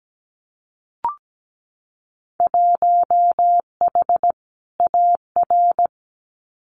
Morse code, heard as a keyed tone, about 17 wpm, 700 Hz. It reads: 1H AR
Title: Winter Field Day Exchanges 17wpm